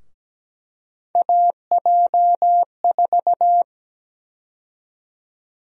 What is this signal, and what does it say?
Morse code, heard as a keyed tone, about 17 wpm, 700 Hz. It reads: AJ4
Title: USA Call Sign Prefixes - Repeated 2x - 17wpm